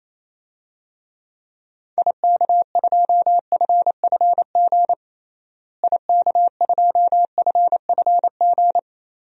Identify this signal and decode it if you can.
Morse code, heard as a keyed tone, about 27 wpm, 700 Hz. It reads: IK2FFG IK2FFG
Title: DXCC-100 Call Signs - 6 Characters - Repeated 2x - 28wpm